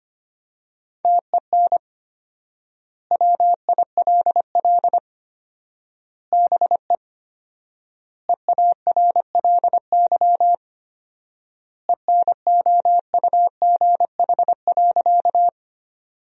Morse code, heard as a keyed tone, about 25 wpm, 700 Hz. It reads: TEN WILL BE EARLY ENOUGH.
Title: 4x Word-Spacing - Sentences from Top 400 Words - 25wpm